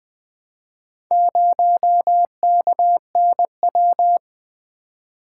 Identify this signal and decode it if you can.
Morse code, heard as a keyed tone, about 20 wpm, 700 Hz. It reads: 0KNW
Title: Four Letter-Number Combinations - 20wpm